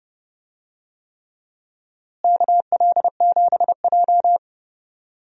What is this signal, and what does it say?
Morse code, heard as a keyed tone, about 31 wpm, 700 Hz. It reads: KL7J
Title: DXCC-100 Call Signs - 4 Characters - Repeated 3x - 30wpm